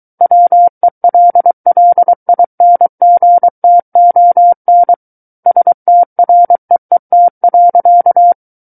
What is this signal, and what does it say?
Morse code, heard as a keyed tone, about 23 wpm, 700 Hz. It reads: WELLINGTON STREET.
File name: Book One / 14-Chapter 14 - 23 WPM